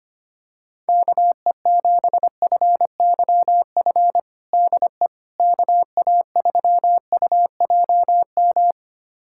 Morse code, heard as a keyed tone, about 25 wpm, 700 Hz. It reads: KE7FYF DE KA3UJM
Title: USA Callsign DE Callsign - 6 Characters - 25wpm